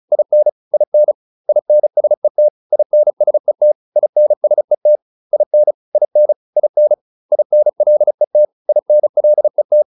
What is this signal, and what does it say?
Morse code, heard as a keyed tone, about 35 wpm, 600 Hz. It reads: IN IN INSET INSET INSET IN IN IN INLET INLET